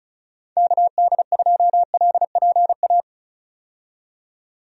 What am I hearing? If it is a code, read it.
Morse code, heard as a keyed tone, about 34 wpm, 700 Hz. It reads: KD2LPA
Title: USA Call Signs - 6 Characters - 35wpm